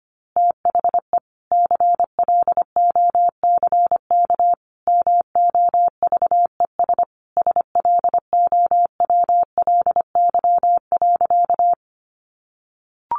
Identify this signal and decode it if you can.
Morse code, heard as a keyed tone, about 25 wpm, 700 Hz. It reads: THE CLOCK MOVES SLOWLY.
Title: Easy Phrases 25wpm